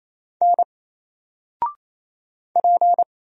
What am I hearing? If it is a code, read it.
Morse code, heard as a keyed tone, about 27 wpm, 700 Hz. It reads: N P
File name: Single-Letter-Number-28wpm